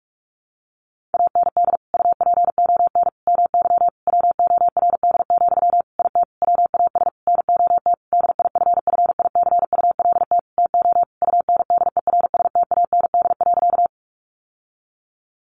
Morse code, heard as a keyed tone, about 44 wpm, 700 Hz. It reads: AND UPON MY WORD, IT WAS NOT DIFFICULT TO UNDERSTAND.
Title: Sentences from Top 1100 Words 45wpm